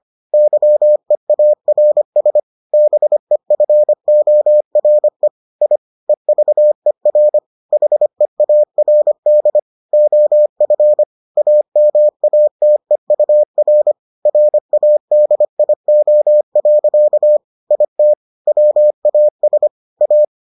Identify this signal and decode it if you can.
Morse code, heard as a keyed tone, about 25 wpm, 600 Hz. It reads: YEARS BEFORE I EVER HEARD OF AMATEUR RADIO. IT WAS A